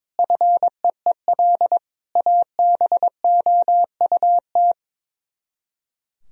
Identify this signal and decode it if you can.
Morse code, heard as a keyed tone, about 22 wpm, 700 Hz. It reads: FEEL ABOUT
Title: Speed-Racing - Sets of 5 Words 22wpm